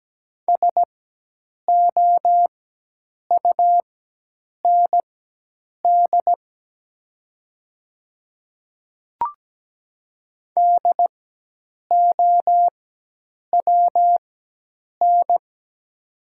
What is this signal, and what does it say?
Morse code, heard as a keyed tone, about 17 wpm, spaced as for 8 wpm, 700 Hz. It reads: SOUND DOWN
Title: 4x Character Spacing - Top 100 Words - 17wpm